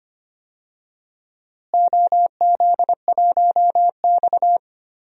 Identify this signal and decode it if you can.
Morse code, heard as a keyed tone, about 25 wpm, 700 Hz. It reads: OZ1X
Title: DXCC-100 Call Signs - 4 Characters - Repeated 2x - 25wpm